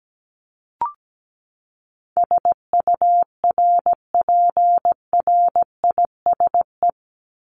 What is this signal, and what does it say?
Morse code, heard as a keyed tone, about 17 wpm, 700 Hz. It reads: SURPRISE
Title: Top 1751-2000 Words 17wpm